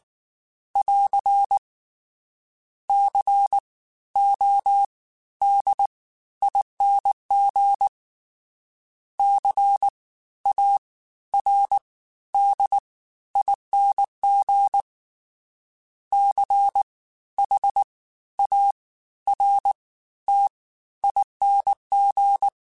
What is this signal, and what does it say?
Morse code, heard as a keyed tone, about 19 wpm, spaced as for 13 wpm, 800 Hz. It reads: <AR> CODING CARDING CHARTING
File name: Special INGs-mp3s / Special INGs-mp3s / Fast ing @18